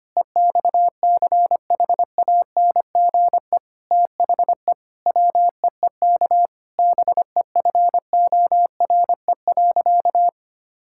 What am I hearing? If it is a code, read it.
Morse code, heard as a keyed tone, about 25 wpm, 700 Hz. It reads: EXCHANGE THE WEEK BEFORE.
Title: Sentences from Top 1000 Words - Review Part 6 - 25wpm